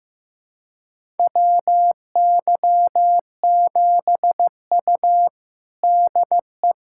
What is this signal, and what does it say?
Morse code, heard as a keyed tone, about 15 wpm, 700 Hz. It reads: WY7U DE
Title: USA Callsign DE Callsign - 4 Characters - 15wpm